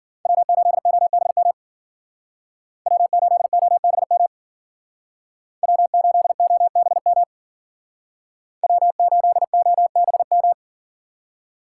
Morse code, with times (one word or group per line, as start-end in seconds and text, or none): 0.3-1.5 W8OBM
2.9-4.3 W8OBM
5.6-7.2 W8OBM
8.6-10.5 W8OBM